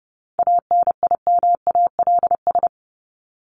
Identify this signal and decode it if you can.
Morse code, heard as a keyed tone, about 30 wpm, 700 Hz. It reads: ANIMALS
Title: ES:S Suffix - Top 100 Words with Base Words 30wpm